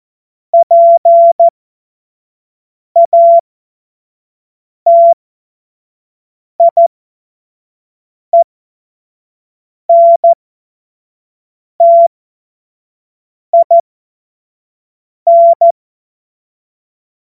Morse code, heard as a keyed tone, about 14 wpm, spaced as for 5 wpm, 680 Hz. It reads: PATIENTIN
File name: Morsekurs / Witze / Witz 06 Februar 2024 25 F70